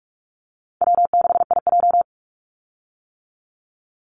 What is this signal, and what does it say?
Morse code, heard as a keyed tone, about 44 wpm, 700 Hz. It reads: W6IJ
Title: Easy Call Signs - 45wpm